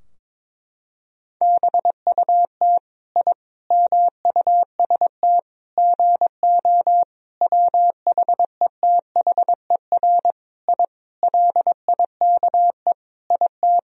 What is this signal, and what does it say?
Morse code, heard as a keyed tone, about 22 wpm, 700 Hz. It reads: BUT I MUST GO WHETHER I LIKE IT
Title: Sentences from Top 1000 Words - Review Part 3 - 22wpm